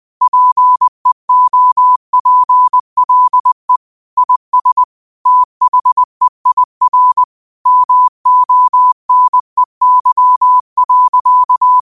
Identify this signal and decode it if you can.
Morse code, heard as a keyed tone, about 20 wpm, 1000 Hz. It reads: PEOPLE IS THEIR MONEY.